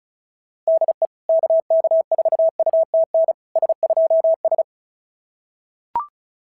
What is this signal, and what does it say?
Morse code, heard as a keyed tone, about 33 wpm, 650 Hz. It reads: DE KK4UTN S2S
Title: SOTA QSO Practice 35wpm